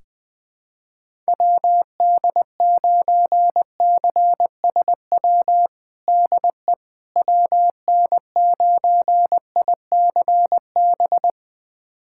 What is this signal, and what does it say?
Morse code, heard as a keyed tone, about 20 wpm, 700 Hz. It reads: WD9CSW DE WN9ICB
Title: USA Callsign DE Callsign 20wpm